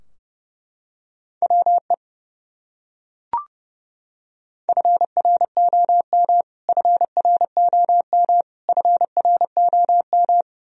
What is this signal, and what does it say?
Morse code, heard as a keyed tone, about 30 wpm, 700 Hz. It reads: WE FROM FROM FROM
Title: Top 100 Words Repeated 30wpm